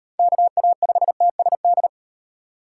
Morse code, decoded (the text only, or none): KA5TSD